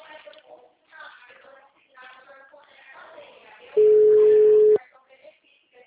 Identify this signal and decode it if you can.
Ringback tone (Europe, 425 Hz).